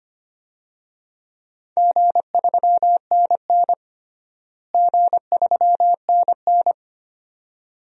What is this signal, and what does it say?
Morse code, heard as a keyed tone, about 25 wpm, 700 Hz. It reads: G3NN G3NN
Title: DXCC-100 Call Signs - 4 Characters - Repeated 2x - 25wpm